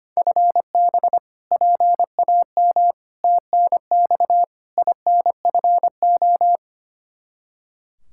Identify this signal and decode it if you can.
Morse code, heard as a keyed tone, about 25 wpm, 700 Hz. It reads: FB PAM TNX INFO
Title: K1USN SST QSO Practice 25wpm